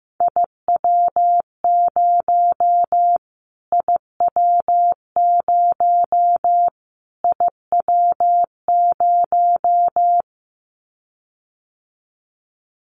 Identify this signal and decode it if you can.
Morse code, heard as a keyed tone, about 15 wpm, 700 Hz. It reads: IW0 IW0 IW0
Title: Callsign Prefixes for Top 25 DXCC - Repeated 3x - 15wpm